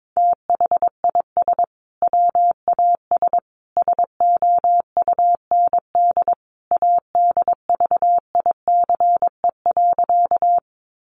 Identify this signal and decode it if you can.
Morse code, heard as a keyed tone, about 22 wpm, 700 Hz. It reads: THIS WAS SOUND ADVICE.
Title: Sentences from Top 1100 Words 22wpm